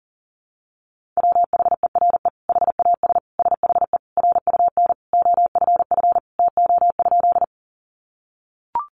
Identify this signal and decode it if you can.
Morse code, heard as a keyed tone, about 40 wpm, 700 Hz. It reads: WHERE HAS SHE RUN OFF TO?
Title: Sentences from Top 300 Words 40wpm